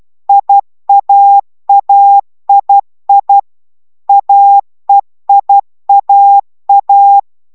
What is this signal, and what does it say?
Morse code, heard as a keyed tone, about 12 wpm, 800 Hz. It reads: IAAII AEIAA
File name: taviro tanulas / cw 69 04 10